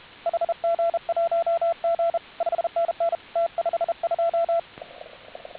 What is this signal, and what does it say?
Morse code, heard as a keyed tone, about 33 wpm, 1340 Hz, with radio noise. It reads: HG1G 5NN T52